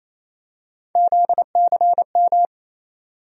Morse code, heard as a keyed tone, about 27 wpm, 700 Hz. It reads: ZCM
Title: USA Call Sign Postfixes - 3 Letters 28wpm